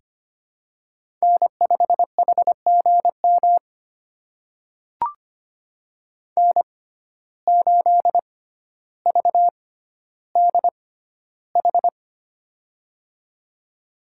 Morse code, heard as a keyed tone, about 25 wpm, spaced as for 13 wpm, 700 Hz. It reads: N5HGM N8VDH
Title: USA Call Signs - 5 Characters - 6x Character Spacing - 25wpm